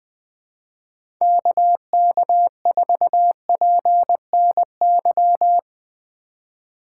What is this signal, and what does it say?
Morse code, heard as a keyed tone, about 20 wpm, 700 Hz. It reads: KK4PNY
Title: USA Call Signs - 2x Character Spacing - 20wpm